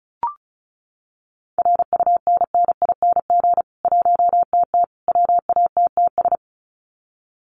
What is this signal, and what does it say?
Morse code, heard as a keyed tone, about 34 wpm, 700 Hz. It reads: RUNNING 1TT WATTS
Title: Common QSO Phrases 35wpm